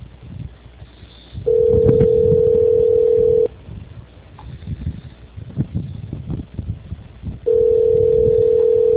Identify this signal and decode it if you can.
Ringback tone (North America), 440 + 480 Hz.
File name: out- -2036- - - .53